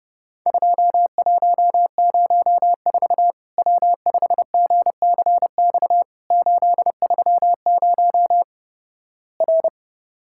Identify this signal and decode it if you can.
Morse code, heard as a keyed tone, about 32 wpm, 700 Hz. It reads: 2104 W5GCX 830 R
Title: Worked All Europe DX Contest - QTC Exchange - 30wpm